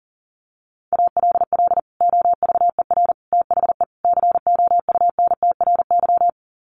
Morse code, heard as a keyed tone, about 37 wpm, 700 Hz. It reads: ALL OVER THE COUNTRY
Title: Sets of 4 Words 40wpm